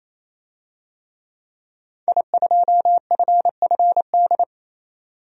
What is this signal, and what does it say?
Morse code, heard as a keyed tone, about 28 wpm, 700 Hz. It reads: I2FFD
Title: DXCC-100 Call Signs - 5 Characters - 28wpm